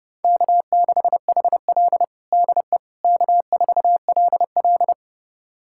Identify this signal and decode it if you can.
Morse code, heard as a keyed tone, about 32 wpm, 700 Hz. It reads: K6HL DE K4LL
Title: USA Callsign DE Callsign - 4 Characters - 30wpm